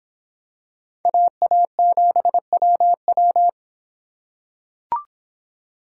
Morse code, heard as a keyed tone, about 24 wpm, 700 Hz. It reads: AA7WW